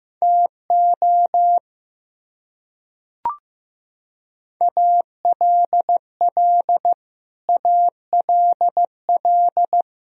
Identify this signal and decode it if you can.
Morse code, heard as a keyed tone, about 16 wpm, 700 Hz. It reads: TO ALL ALL
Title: Top 100 Words Repeated 15wpm